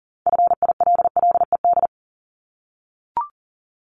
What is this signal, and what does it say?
Morse code, heard as a keyed tone, about 40 wpm, 700 Hz. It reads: FILLED
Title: Multi-letter Phonemes 40wpm